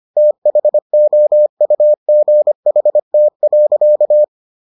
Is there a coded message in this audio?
Yes: Morse code from 0.2 to 4.2 s, heard as a keyed tone, about 25 wpm, 600 Hz. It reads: THOUGHT.